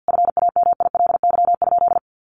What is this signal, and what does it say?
Morse code, heard as a keyed tone, about 48 wpm, 700 Hz. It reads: FAMILY?
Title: Sentences from Top 1300 Words 50wpm